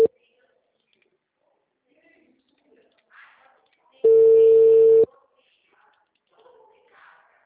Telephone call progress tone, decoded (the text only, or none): ringback tone